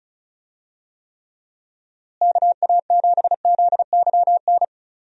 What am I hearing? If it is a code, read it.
Morse code, heard as a keyed tone, about 36 wpm, 700 Hz. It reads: KA7ZYN
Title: USA Call Signs - 6 Characters - Repeated 2x - 35wpm